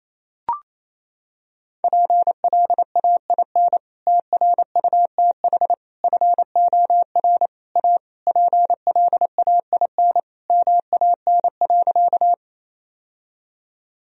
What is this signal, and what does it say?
Morse code, heard as a keyed tone, about 28 wpm, 700 Hz. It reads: PLAIN TRUTH FOR A PLAIN MAN.
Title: Sentences from Top 1000 Words - Review Part 2 - 28wpm